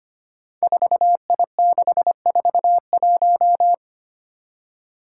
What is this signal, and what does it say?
Morse code, heard as a keyed tone, about 25 wpm, 700 Hz. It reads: 4I641
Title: Five Challenging Letter-Number Combinations 25wpm